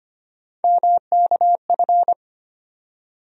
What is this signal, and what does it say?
Morse code, heard as a keyed tone, about 25 wpm, 700 Hz. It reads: MKF
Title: USA Call Sign Postfixes - 3 Letters 25wpm